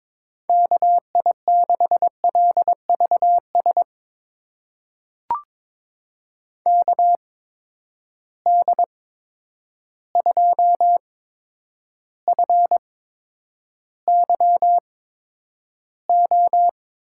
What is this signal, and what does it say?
Morse code, heard as a keyed tone, about 22 wpm, spaced as for 9 wpm, 700 Hz. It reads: KI6LVS KD2FYO
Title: USA Call Signs - 6 Characters - 8x Character Spacing - 22wpm